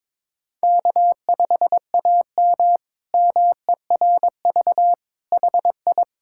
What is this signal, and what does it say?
Morse code, heard as a keyed tone, about 21 wpm, 700 Hz. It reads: K5AM MERV HI
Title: K1USN SST QSO Practice - 22wpm